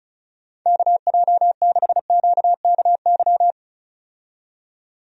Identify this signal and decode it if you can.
Morse code, heard as a keyed tone, about 36 wpm, 700 Hz. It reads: KJ6QKY
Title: USA Call Signs - 6 Characters - 35wpm